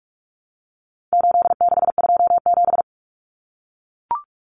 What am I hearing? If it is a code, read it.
Morse code, heard as a keyed tone, about 44 wpm, 700 Hz. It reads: 8627
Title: Numbers - 4-Digit 45wpm